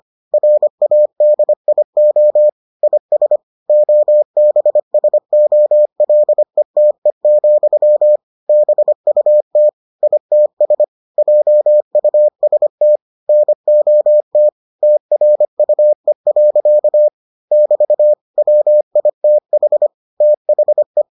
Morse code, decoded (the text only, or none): RADIO IS OBSOLETE, BUT ITS JUST NOT TRUE. <BT> WITH THE